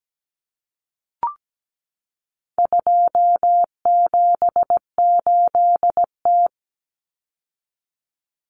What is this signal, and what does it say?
Morse code, heard as a keyed tone, about 17 wpm, 700 Hz. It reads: 278T
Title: Common Cut Numbers - 4-Digit 17wpm